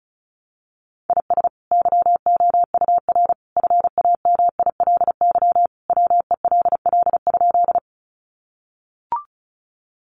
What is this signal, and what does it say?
Morse code, heard as a keyed tone, about 34 wpm, 700 Hz. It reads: IS YOUR FAMILY WELL?
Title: Sentences from Top 400 Words 35wpm